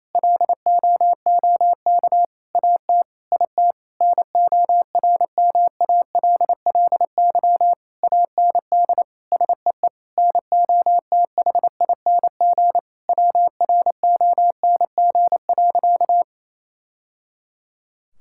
Morse code, heard as a keyed tone, about 28 wpm, 700 Hz. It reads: LOOK AT IT NORMALLY AND SEE NOTHING WRONG.
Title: Sentences from Top 1000 Words - Review Part 6 - 28wpm